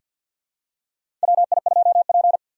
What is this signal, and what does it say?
Morse code, heard as a keyed tone, about 48 wpm, 700 Hz. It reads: WI2P
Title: USA Call Signs - 4 Characters - 3x Character Spacing - 50wpm